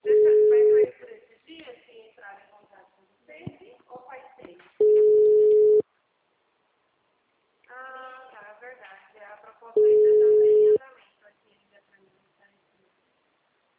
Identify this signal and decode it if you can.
Ringback tone (Europe, 425 Hz), with speech.